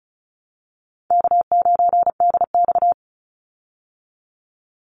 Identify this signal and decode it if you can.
Morse code, heard as a keyed tone, about 34 wpm, 700 Hz. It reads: K9DX
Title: Call Signs Repeated 3x - 35wpm